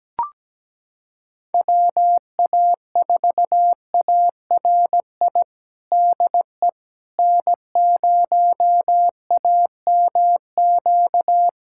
Morse code, heard as a keyed tone, about 17 wpm, 700 Hz. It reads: WA4ARI DE N0AMQ
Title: USA Callsign DE Callsign 17wpm